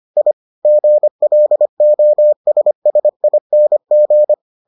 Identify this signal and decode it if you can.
Morse code, heard as a keyed tone, about 25 wpm, 600 Hz. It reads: I GLOSSING